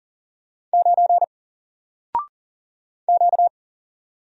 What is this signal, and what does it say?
Morse code, heard as a keyed tone, about 38 wpm, 700 Hz. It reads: 9 Q
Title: Single Character (Letter:Number:Punctuation:Prosign) 40wpm